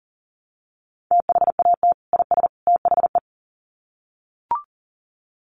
Morse code, heard as a keyed tone, about 40 wpm, 700 Hz. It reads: THAT IS THE
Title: Sets of 3 Words 40wpm